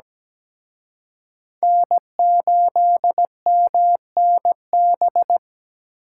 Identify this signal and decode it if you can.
Morse code, heard as a keyed tone, about 20 wpm, 700 Hz. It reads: N8MNB
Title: Speed Racing - USA 1x3 Call Signs 17wpm